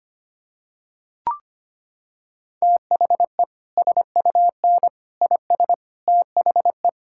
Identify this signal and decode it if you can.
Morse code, heard as a keyed tone, about 25 wpm, 700 Hz. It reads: THE SUN IS THE
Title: Sentences from Top 1000 Words - Review Part 5 - 25wpm